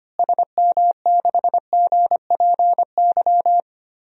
Morse code, heard as a keyed tone, about 25 wpm, 700 Hz. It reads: SM6GPY
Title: DXCC-100 Call Signs - 6 Characters - Repeated 3x - 25wpm